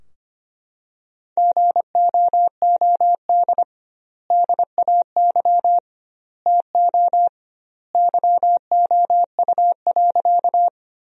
Morse code, heard as a keyed tone, about 25 wpm, 700 Hz. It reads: GOOD DAY TO YOU.
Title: 2x Word-Spacing - Sentences from Top 200 Words - 25wpm